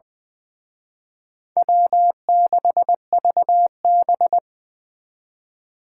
Morse code, heard as a keyed tone, about 23 wpm, 700 Hz. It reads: W6VB
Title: Speed Racing - USA Call Signs 20wpm